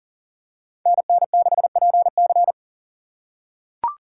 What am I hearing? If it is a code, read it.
Morse code, heard as a keyed tone, about 40 wpm, 700 Hz. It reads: NN6PC
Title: USA Call Signs - 3x Character Spacing - 40wpm